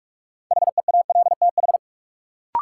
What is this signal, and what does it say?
Morse code, heard as a keyed tone, about 46 wpm, 700 Hz. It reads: HEALTH
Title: Sentences from Top 600 Words - New Word Drill - 45wpm